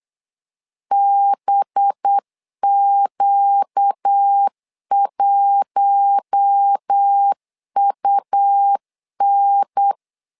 Morse code, heard as a keyed tone, about 8 wpm, 800 Hz. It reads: BQ1UN
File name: LEZIONI / 5 LBVC / 5 LBVC 8wpm 2935